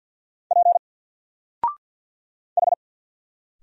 Morse code, heard as a keyed tone, about 50 wpm, 700 Hz. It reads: P H
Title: Single Character (Letter:Number:Punctuation:Prosign) 50wpm